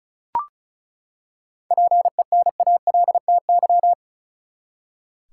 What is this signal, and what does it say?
Morse code, heard as a keyed tone, about 34 wpm, 700 Hz. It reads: PENALTY
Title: Seven Letter Words - 35wpm